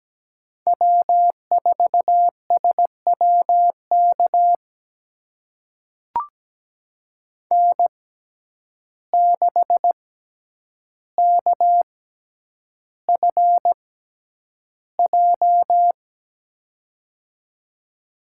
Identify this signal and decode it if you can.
Morse code, heard as a keyed tone, about 17 wpm, spaced as for 9 wpm, 700 Hz. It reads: W4SWK N6KFJ
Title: USA Call Signs - 5 Characters - 6x Character Spacing - 17wpm